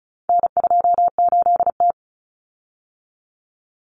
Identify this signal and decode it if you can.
Morse code, heard as a keyed tone, about 34 wpm, 700 Hz. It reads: N28T
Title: Common Cut Numbers - 4-Digit 35wpm